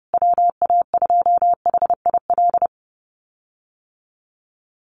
Morse code, heard as a keyed tone, about 30 wpm, 700 Hz. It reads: WA2HIL
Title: USA Call Signs Repeated 3x 30wpm